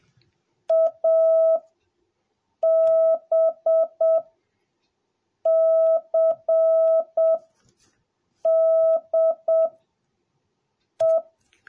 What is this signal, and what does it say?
Morse code, heard as a keyed tone, about 7 wpm, spaced as for 5 wpm, 650 Hz. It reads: ABCDE